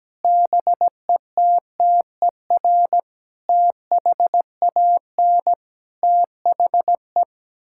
Morse code, heard as a keyed tone, about 17 wpm, 700 Hz. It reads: BETTER THAN THE
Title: Sentences from Top 1000 Words - Review Part 6 - 17wpm